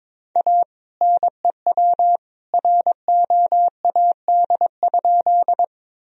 Morse code, heard as a keyed tone, about 21 wpm, 700 Hz. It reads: A NEW ROAD?
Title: Sentences from Top 1000 Words - Review Part 2 - 22wpm